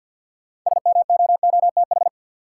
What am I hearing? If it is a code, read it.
Morse code, heard as a keyed tone, about 52 wpm, 700 Hz. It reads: SMOOTH
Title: Sentences from Top 1400 Words - New Word Drill - 50wpm